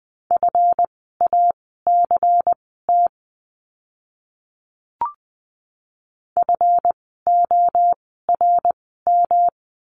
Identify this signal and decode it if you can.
Morse code, heard as a keyed tone, about 20 wpm, spaced as for 15 wpm, 700 Hz. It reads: FACT FORM
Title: 139 Lesson 23f - Letter f - Words 20wpm